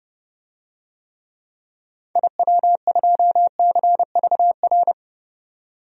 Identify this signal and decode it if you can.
Morse code, heard as a keyed tone, about 30 wpm, 700 Hz. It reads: IW2CVR
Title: DXCC-100 Call Signs - 6 Characters - 30wpm